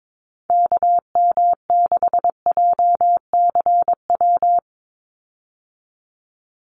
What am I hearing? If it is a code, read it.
Morse code, heard as a keyed tone, about 22 wpm, 700 Hz. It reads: KM6JCW
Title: USA Call Signs 22wpm